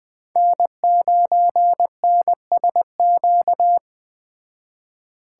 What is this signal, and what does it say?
Morse code, heard as a keyed tone, about 20 wpm, 700 Hz. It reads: N9NSQ